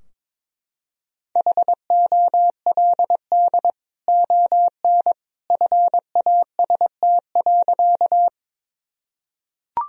Morse code, heard as a keyed tone, about 22 wpm, 700 Hz. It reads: HOLD ON FAST.
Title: Sentences from Top 1000 Words - Review Part 2 - 22wpm